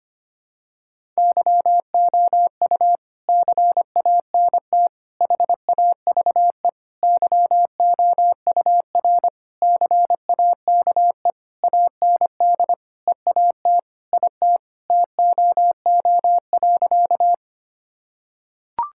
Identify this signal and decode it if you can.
Morse code, heard as a keyed tone, about 25 wpm, 700 Hz. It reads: YOU CANT HAVE YOUR CAKE AND EAT IT TOO.
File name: practice / 25wpm / Sentences with 9-Word Phrases - 25wpm